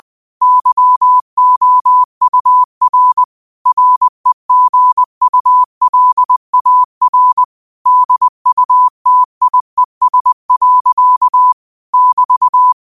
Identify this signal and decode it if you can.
Morse code, heard as a keyed tone, about 20 wpm, 1000 Hz. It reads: YOUR REGULAR DUTIES. <BT>
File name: qotdfiles / 20wpm